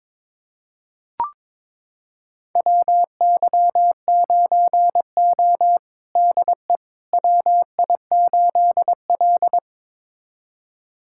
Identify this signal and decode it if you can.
Morse code, heard as a keyed tone, about 22 wpm, 700 Hz. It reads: WY9O DE WI8L
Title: USA Callsign DE Callsign - 4 Characters - 22wpm